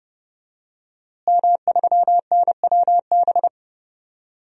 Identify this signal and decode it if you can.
Morse code, heard as a keyed tone, about 32 wpm, 700 Hz. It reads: M3NWB
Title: DXCC-100 Call Signs - 5 Characters - Repeated 3x - 30wpm